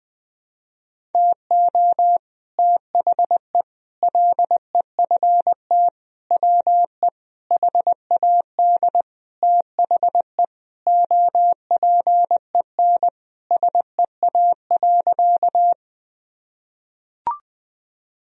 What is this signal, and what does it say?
Morse code, heard as a keyed tone, about 21 wpm, 700 Hz. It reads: TO THE LEFT WE HAD THE OPEN SEA.
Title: Sentences from Top 1000 Words - Review Part 1 - 20wpm